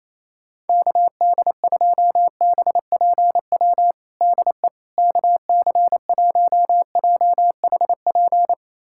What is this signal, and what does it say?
Morse code, heard as a keyed tone, about 28 wpm, 700 Hz. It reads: KD2BPW DE KC1JHP
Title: USA Callsign DE Callsign 28wpm